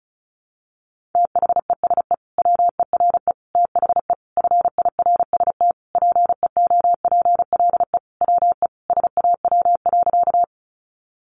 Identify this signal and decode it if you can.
Morse code, heard as a keyed tone, about 34 wpm, 700 Hz. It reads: THESE WERE THE FIRST PEOPLE WE SAW.
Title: Sentences from Top 300 Words 35wpm